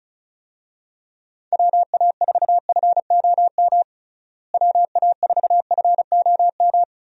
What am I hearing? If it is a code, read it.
Morse code, heard as a keyed tone, about 36 wpm, 700 Hz. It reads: WA4FOM WA4FOM
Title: USA Call Signs - 6 Characters - Repeated 3x - 35wpm